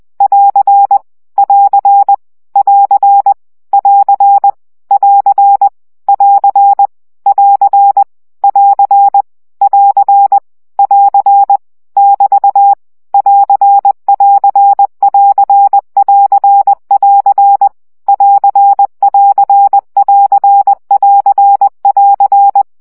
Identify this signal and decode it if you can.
Morse code, heard as a keyed tone, about 20 wpm, 800 Hz. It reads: <AR> <AR> <AR> <AR> <AR> <AR> <AR> <AR> <AR> <AR> <BT> <AR><AR><AR><AR><AR> <AR><AR><AR><AR><AR>
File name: CW gyakorlas plusz 17 baud